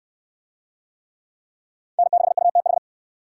Morse code, heard as a keyed tone, about 68 wpm, 700 Hz. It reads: N6UTH